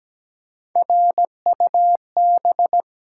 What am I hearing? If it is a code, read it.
Morse code, heard as a keyed tone, about 17 wpm, 700 Hz. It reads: RUB
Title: Three Letter Words - 17wpm